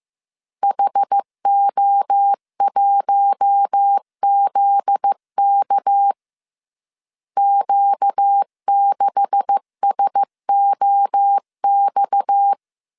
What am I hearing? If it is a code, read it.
Morse code, heard as a keyed tone, about 15 wpm, 800 Hz. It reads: HO1ZK Q6SOX
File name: CW / DET / 1 9 1500C S C 14wpm 3245